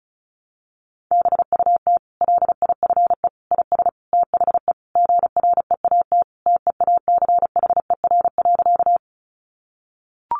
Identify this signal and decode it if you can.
Morse code, heard as a keyed tone, about 34 wpm, 700 Hz. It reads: BUT LIFE IS THE GREAT TEACHER.
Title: Sentences from Top 900 Words 35wpm